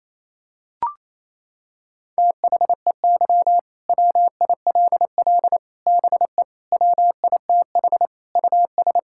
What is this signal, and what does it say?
Morse code, heard as a keyed tone, about 28 wpm, 700 Hz. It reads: THEY WILL BE WITH US
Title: Sentences from Top 1600 Words 28wpm